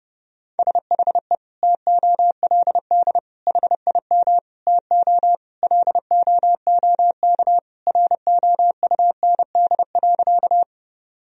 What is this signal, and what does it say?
Morse code, heard as a keyed tone, about 30 wpm, 700 Hz. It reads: SHE TOLD HIM TO LOOK ROUND.
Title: Sentences from Top 1000 Words - Review Part 1 - 30wpm